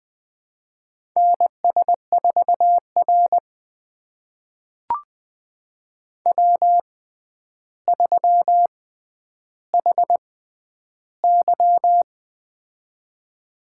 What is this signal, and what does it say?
Morse code, heard as a keyed tone, about 20 wpm, spaced as for 10 wpm, 700 Hz. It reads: NS4R W3HY
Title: USA Call Signs - 4 Characters - 6x Character Spacing - 20wpm